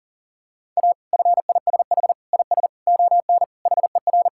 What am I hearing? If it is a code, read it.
Morse code, heard as a keyed tone, about 40 wpm, 700 Hz. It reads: A FISH IS ON HER
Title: Sentences from Top 1600 Words 40wpm